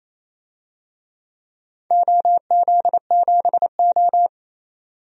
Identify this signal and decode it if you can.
Morse code, heard as a keyed tone, about 27 wpm, 690 Hz. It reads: OZ7O
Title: DXCC-100 Call Signs - 4 Characters - 28wpm